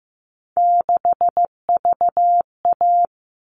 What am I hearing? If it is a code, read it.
Morse code, heard as a keyed tone, about 15 wpm, 700 Hz. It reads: 6VA
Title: All Cut Numbers - 3-Digit 15wpm